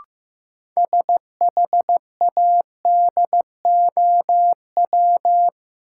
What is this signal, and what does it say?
Morse code, heard as a keyed tone, about 15 wpm, 700 Hz. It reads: SHADOW
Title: Sets of 5 Words Encore - New Word Drill - 15wpm